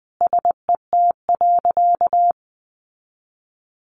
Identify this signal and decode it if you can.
Morse code, heard as a keyed tone, about 20 wpm, 700 Hz. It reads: SET.
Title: Sentences from Top 400 Words 20wpm